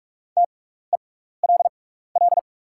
Morse code, heard as a keyed tone, about 44 wpm, spaced as for 15 wpm, 690 Hz. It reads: TELL
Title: 6x Character Spacing - Top 200 Words - 45wpm